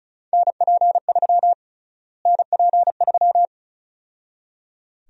Morse code, heard as a keyed tone, about 36 wpm, 700 Hz. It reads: NP3 NP3
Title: USA Call Sign Prefixes - Repeated 2x - 35wpm